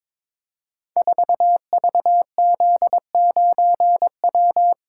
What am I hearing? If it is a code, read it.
Morse code, heard as a keyed tone, about 23 wpm, 700 Hz. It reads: 4VZ9W
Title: Code Groups - 5 Letter:Numbers - 22wpm